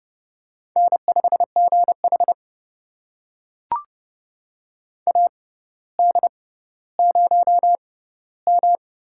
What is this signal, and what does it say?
Morse code, heard as a keyed tone, about 29 wpm, spaced as for 15 wpm, 700 Hz. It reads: N5GH AD0M
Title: USA Call Signs - 4 Characters - 6x Character Spacing - 30wpm